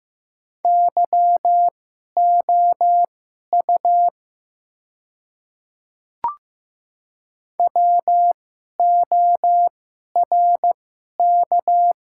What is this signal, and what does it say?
Morse code, heard as a keyed tone, about 15 wpm, spaced as for 11 wpm, 700 Hz. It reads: YOU WORK
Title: 2x Character Spacing - Top 200 Words - 15wpm